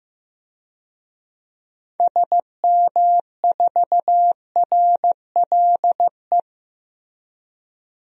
Morse code, heard as a keyed tone, about 15 wpm, 700 Hz. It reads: SM4RLE
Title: DXCC-100 Call Signs - 6 Characters - 15wpm